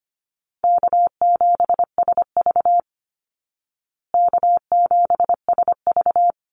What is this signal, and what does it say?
Morse code, heard as a keyed tone, about 25 wpm, 700 Hz. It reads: K7SV K7SV
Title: Easy Call Signs Repeated 2x - 25wpm